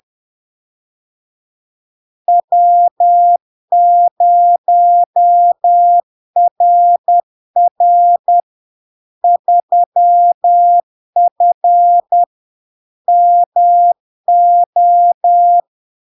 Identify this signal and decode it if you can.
Morse code, heard as a keyed tone, about 10 wpm, 700 Hz. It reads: W0RR 3F MO
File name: LARK-FD-2022-Log-in-CW-10wpm-700-Hz